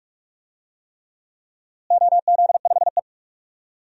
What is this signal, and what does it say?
Morse code, heard as a keyed tone, about 43 wpm, 700 Hz. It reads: OZ5E
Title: DXCC-100 Call Signs - 4 Characters - Repeated 3x - 45wpm